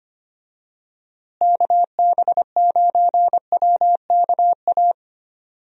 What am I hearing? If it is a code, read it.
Morse code, heard as a keyed tone, about 25 wpm, 700 Hz. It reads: KB9WKA
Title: USA Call Signs - 6 Characters - Repeated 3x - 25wpm